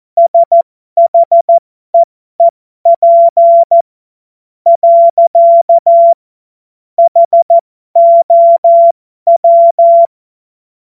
Morse code, heard as a keyed tone, about 14 wpm, spaced as for 11 wpm, 680 Hz. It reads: SHEEP. HOW
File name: Morsekurs / Witze